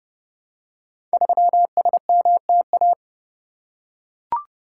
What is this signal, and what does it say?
Morse code, heard as a keyed tone, about 28 wpm, 700 Hz. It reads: 3SMTA
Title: Code Groups - 5 Letter:Numbers - 30wpm